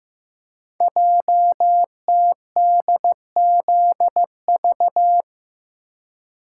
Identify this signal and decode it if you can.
Morse code, heard as a keyed tone, about 15 wpm, 700 Hz. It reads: JTDZV
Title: Code Groups - 5 Letters - 15wpm